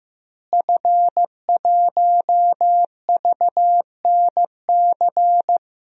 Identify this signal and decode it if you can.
Morse code, heard as a keyed tone, about 15 wpm, 700 Hz. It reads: F1VNC
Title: DXCC-100 Call Signs - 5 Characters - Repeated 3x - 15wpm